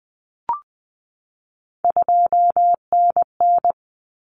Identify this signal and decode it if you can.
Morse code, heard as a keyed tone, about 20 wpm, 700 Hz. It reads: 2NN
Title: Common Cut Numbers - 3-Digit 20wpm